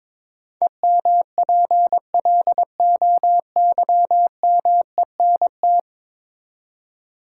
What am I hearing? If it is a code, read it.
Morse code, heard as a keyed tone, about 22 wpm, 700 Hz. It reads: EMPLOYMENT
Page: morse-code-ninja-practice-22wpm